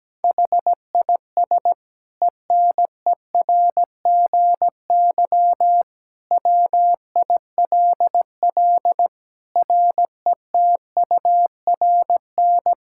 Morse code, heard as a keyed tone, about 17 wpm, 700 Hz. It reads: HIS ENERGY WILL RETURN
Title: Sentences from Top 1000 Words - Review Part 6 - 17wpm